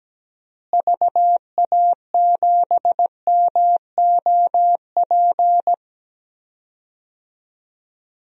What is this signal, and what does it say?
Morse code, heard as a keyed tone, about 17 wpm, 700 Hz. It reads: VA7MOP
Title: DXCC-100 Call Signs - 6 Characters - Repeated 3x - 17wpm